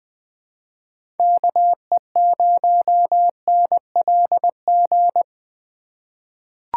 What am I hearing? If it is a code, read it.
Morse code, heard as a keyed tone, about 20 wpm, 700 Hz. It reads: KE0NLG
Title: USA Call Signs - 6 Characters - 3x Character Spacing - 20wpm